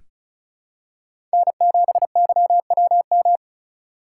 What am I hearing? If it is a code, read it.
Morse code, heard as a keyed tone, about 36 wpm, 700 Hz. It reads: N7YWM
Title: USA Call Signs - 5 Characters - 2x Character Spacing - 35wpm